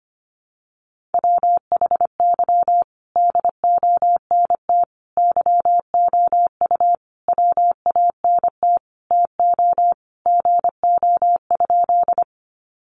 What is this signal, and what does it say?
Morse code, heard as a keyed tone, about 25 wpm, 700 Hz. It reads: WHY DONT YOU WANT TO GO?
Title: Sentences from Top 300 Words 25wpm